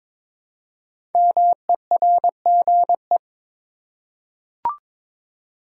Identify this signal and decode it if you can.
Morse code, heard as a keyed tone, about 22 wpm, 690 Hz. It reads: MERGE
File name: 5 Letter Words - First 3 Letters Given - 22wpm